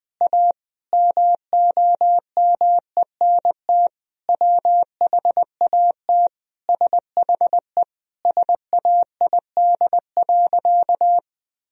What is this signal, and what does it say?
Morse code, heard as a keyed tone, about 20 wpm, 700 Hz. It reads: A MOMENT WHAT SHE SAID.
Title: Sentences from Top 1000 Words - Review Part 4 - 20wpm